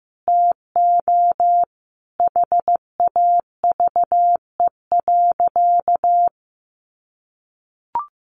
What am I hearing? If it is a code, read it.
Morse code, heard as a keyed tone, about 15 wpm, 700 Hz. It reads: TO HAVE.
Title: Sentences from Top 1200 Words 15wpm